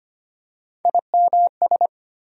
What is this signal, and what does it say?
Morse code, heard as a keyed tone, about 25 wpm, 700 Hz. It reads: IMS
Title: USA Call Sign Postfixes - 3 Letters 25wpm